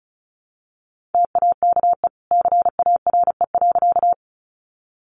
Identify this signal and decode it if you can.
Morse code, heard as a keyed tone, about 34 wpm, 700 Hz. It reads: TAKE CARE.
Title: Sentences from Top 300 Words 35wpm